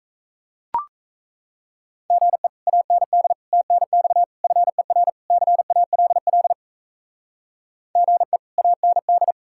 Morse code, heard as a keyed tone, about 36 wpm, 700 Hz. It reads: GE AND TNX FER CALL GE AND